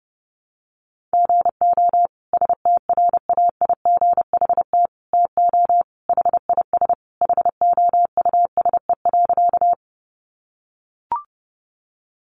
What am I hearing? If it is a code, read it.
Morse code, heard as a keyed tone, about 30 wpm, 700 Hz. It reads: GO STRAIGHT TO HIS HOUSE.
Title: Sentences from Top 1000 Words 30wpm